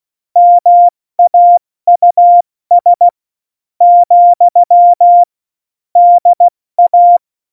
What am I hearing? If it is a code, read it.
Morse code, heard as a keyed tone, about 16 wpm, spaced as for 13 wpm, 700 Hz. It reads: MAUS, DA
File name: Morsekurs / Witze / Witz 18. September 2018, 70 F80